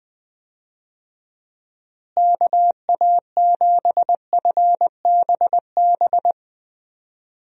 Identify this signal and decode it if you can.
Morse code, heard as a keyed tone, about 21 wpm, 700 Hz. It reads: KA7FBB
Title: USA Call Signs - 6 Characters - Repeated 2x - 20wpm